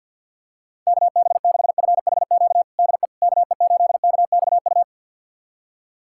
Morse code, heard as a keyed tone, about 50 wpm, 700 Hz. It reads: KB6FHQ DE KE8KXU
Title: USA Callsign DE Callsign 50wpm